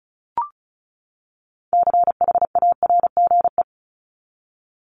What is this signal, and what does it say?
Morse code, heard as a keyed tone, about 34 wpm, 700 Hz. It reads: CHARGE
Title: Top 751-1000 Words 35wpm